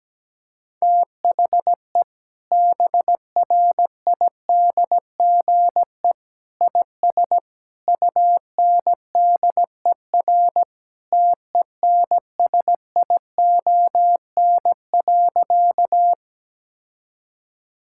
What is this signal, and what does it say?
Morse code, heard as a keyed tone, about 17 wpm, 700 Hz. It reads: THE BRIDGE IS UNDER TENSION.
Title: Sentences from Top 1600 Words 17wpm